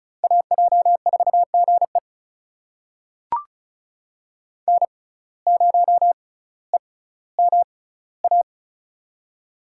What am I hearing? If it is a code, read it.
Morse code, heard as a keyed tone, about 36 wpm, spaced as for 18 wpm, 700 Hz. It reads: AJ4GE N0EMA
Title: USA Call Signs - 5 Characters - 6x Character Spacing - 35wpm